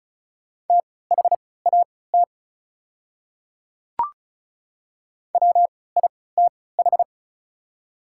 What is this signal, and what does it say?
Morse code, heard as a keyed tone, about 34 wpm, spaced as for 20 wpm, 690 Hz. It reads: THAT WITH
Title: 3x Character Spacing - Top 100 Words - 35wpm